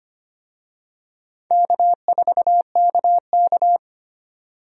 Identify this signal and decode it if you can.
Morse code, heard as a keyed tone, about 25 wpm, 700 Hz. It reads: K4KK
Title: USA Call Signs - 4 Characters - Repeated 3x - 25wpm